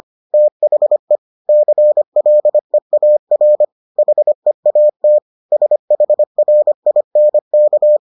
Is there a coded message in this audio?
Yes: Morse code from 0.3 to 8.0 s, heard as a keyed tone, about 25 wpm, 600 Hz. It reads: THE CLEAR HEAT SHRINK